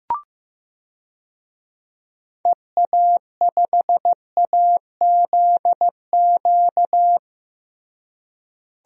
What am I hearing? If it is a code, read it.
Morse code, heard as a keyed tone, about 16 wpm, 700 Hz. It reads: EA5AZQ
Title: DXCC-100 Call Signs - 6 Characters - Repeated 3x - 15wpm